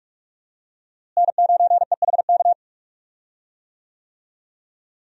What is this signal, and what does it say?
Morse code, heard as a keyed tone, about 43 wpm, 700 Hz. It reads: N9EHK
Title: USA Call Signs - 5 Characters - Repeated 3x - 45wpm